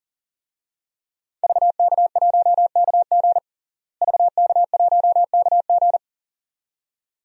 Morse code, heard as a keyed tone, about 37 wpm, 700 Hz. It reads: VK1KG VK1KG
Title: DXCC-100 Call Signs - 5 Characters - Repeated 2x - 40wpm